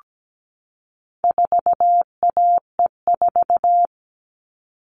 Morse code, heard as a keyed tone, about 17 wpm, 700 Hz. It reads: 4AE4
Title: All Cut Numbers - 4-Digit 17wpm